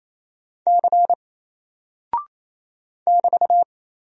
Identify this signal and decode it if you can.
Morse code, heard as a keyed tone, about 28 wpm, 700 Hz. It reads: C <BT>
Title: Single Character (Letter:Number:Punctuation:Prosign) Alt 28wpm